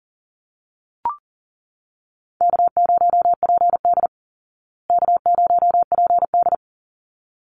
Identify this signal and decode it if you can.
Morse code, heard as a keyed tone, about 40 wpm, 700 Hz. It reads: K0PD K0PD
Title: Easy Call Signs Repeated 2x - 40wpm